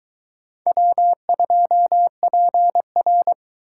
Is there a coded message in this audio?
Yes: Morse code from 0.7 to 3.3 s, heard as a keyed tone, about 22 wpm, 700 Hz. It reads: W2PR